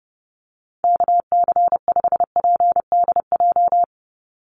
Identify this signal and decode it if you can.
Morse code, heard as a keyed tone, about 28 wpm, 700 Hz. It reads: KC5PDJ
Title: USA Call Signs 30wpm